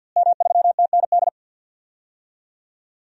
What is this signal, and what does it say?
Morse code, heard as a keyed tone, about 51 wpm, 700 Hz. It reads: M3TND
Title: DXCC-100 Call Signs - 5 Characters - Repeated 2x - 50wpm